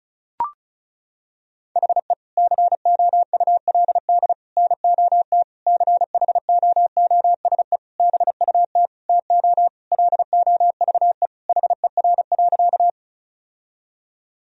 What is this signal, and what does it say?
Morse code, heard as a keyed tone, about 34 wpm, 700 Hz. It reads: HE COULD NOT CHOOSE BUT TO LOVE HER.
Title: Sentences from Top 1000 Words - Review Part 4 - 35wpm